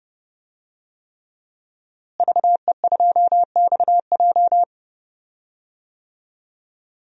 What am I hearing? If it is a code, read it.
Morse code, heard as a keyed tone, about 30 wpm, 700 Hz. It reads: VE2XJ
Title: DXCC-100 Call Signs - 5 Characters - Repeated 3x - 30wpm